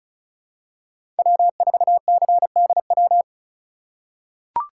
Morse code, heard as a keyed tone, about 36 wpm, 700 Hz. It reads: W4CDW
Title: USA Call Signs - 5 Characters - 3x Character Spacing - 35wpm